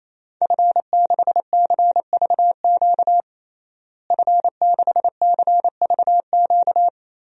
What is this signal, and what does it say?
Morse code, heard as a keyed tone, about 28 wpm, 700 Hz. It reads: F6CVQ F6CVQ
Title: DXCC-100 Call Signs - 5 Characters - Repeated 3x - 28wpm